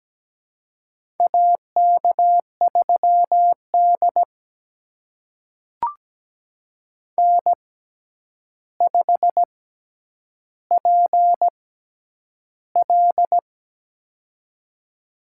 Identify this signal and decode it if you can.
Morse code, heard as a keyed tone, about 17 wpm, spaced as for 9 wpm, 700 Hz. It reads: AK3D N5PL
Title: USA Call Signs - 4 Characters - 6x Character Spacing - 17wpm